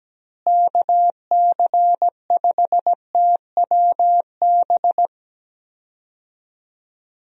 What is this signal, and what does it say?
Morse code, heard as a keyed tone, about 17 wpm, 700 Hz. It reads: KC5TWB
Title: USA Call Signs - 6 Characters - Repeated 2x - 17wpm